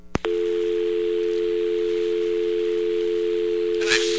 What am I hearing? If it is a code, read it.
Dial tone (North America), 350 + 440 Hz.